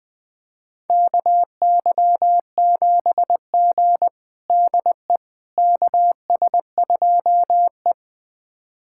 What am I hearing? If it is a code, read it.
Morse code, heard as a keyed tone, about 20 wpm, 700 Hz. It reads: KY7G DE KS2E
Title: USA Callsign DE Callsign - 4 Characters - 20wpm